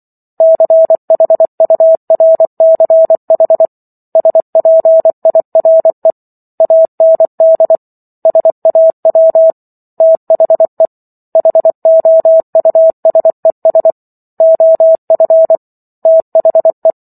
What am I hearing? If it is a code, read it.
Morse code, heard as a keyed tone, about 24 wpm, 650 Hz. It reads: CHURCH SPIRE AND SAW THE HOUSES OF THE